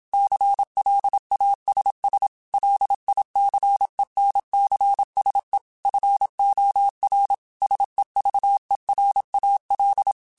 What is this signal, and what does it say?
Morse code, heard as a keyed tone, about 27 wpm, 800 Hz. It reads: CLASS LICENCSE FOR SEVERAL
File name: QSOs / QSOs-mp3s / kc7sac@25